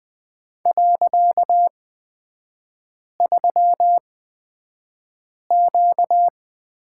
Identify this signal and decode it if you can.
Morse code, heard as a keyed tone, about 20 wpm, spaced as for 5 wpm, 700 Hz. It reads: .3Q
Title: 255 Lesson 41e - Period - Three Characters 20wpm